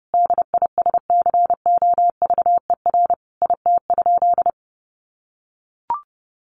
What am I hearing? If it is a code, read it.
Morse code, heard as a keyed tone, about 30 wpm, 700 Hz. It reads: DISCOVER IT?
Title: Sentences from Top 1100 Words 30wpm